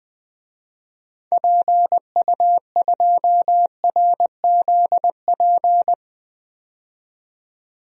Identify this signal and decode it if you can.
Morse code, heard as a keyed tone, about 20 wpm, 700 Hz. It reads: PU2RZP
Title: DXCC-100 Call Signs - 6 Characters - 20wpm